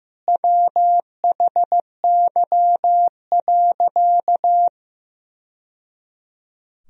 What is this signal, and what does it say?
Morse code, heard as a keyed tone, about 15 wpm, 700 Hz. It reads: WHY.
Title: Sentences from Top 1000 Words - Review Part 3 - 15wpm